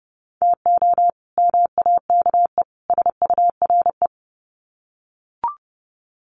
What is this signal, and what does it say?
Morse code, heard as a keyed tone, about 30 wpm, 700 Hz. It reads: TO MAKE SURE
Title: Sets of 3 Words 30wpm